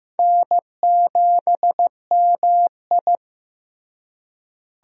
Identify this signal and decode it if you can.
Morse code, heard as a keyed tone, about 15 wpm, 700 Hz. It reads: N7MI
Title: USA Call Signs - 4 Characters - Repeated 2x - 15wpm